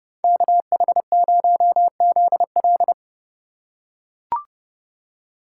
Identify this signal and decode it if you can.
Morse code, heard as a keyed tone, about 29 wpm, 700 Hz. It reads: KH0ZL
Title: USA Call Signs - 5 Characters - 4x Character Spacing - 30wpm